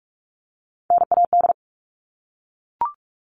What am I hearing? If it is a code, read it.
Morse code, heard as a keyed tone, about 44 wpm, 700 Hz. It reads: NAD
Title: All Cut Numbers - 3-Digit 45wpm